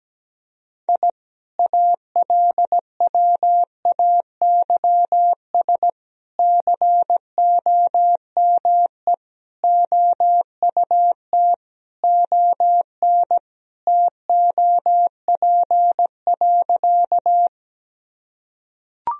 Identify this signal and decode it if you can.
Morse code, heard as a keyed tone, about 17 wpm, 700 Hz. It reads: I ALWAYS COME OUT ON TOP.
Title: Sentences from Top 1000 Words - Review Part 2 - 17wpm